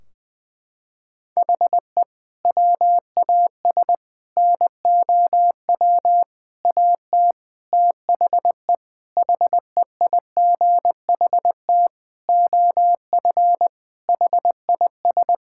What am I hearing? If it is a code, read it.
Morse code, heard as a keyed tone, about 20 wpm, 700 Hz. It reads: HE WAS NOW AT THE HEIGHT OF HIS
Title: Sentences from Top 1600 Words 20wpm